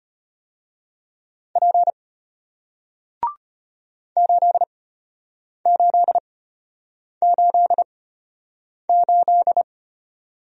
Morse code, heard as a keyed tone, about 33 wpm, spaced as for 8 wpm, 700 Hz. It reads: P 8888